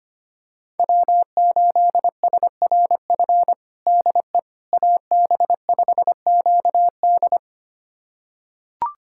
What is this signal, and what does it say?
Morse code, heard as a keyed tone, about 25 wpm, 700 Hz. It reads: W8SRF DE AB5QD
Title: USA Callsign DE Callsign - 5 Characters - 25wpm